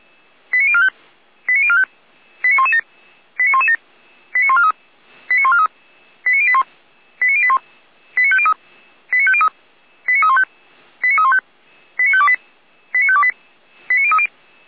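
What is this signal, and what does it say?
Five-tone selective call (ZVEI), heard as quick runs of five tones, tones of about 70 ms. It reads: 89045 89045 89117 89117 89143 89143 89081 89081 89503 89503 89315 89315 89528 89528 89030
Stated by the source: ZVEI five-tone call received on SDR.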